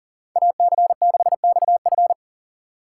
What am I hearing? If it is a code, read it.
Morse code, heard as a keyed tone, about 40 wpm, 700 Hz. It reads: AC6XF
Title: USA Call Signs - 5 Characters - 4x Character Spacing - 40wpm